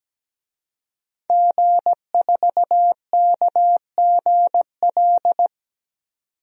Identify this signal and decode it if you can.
Morse code, heard as a keyed tone, about 17 wpm, 700 Hz. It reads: G4KGL
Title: DXCC-100 Call Signs - 5 Characters - 17wpm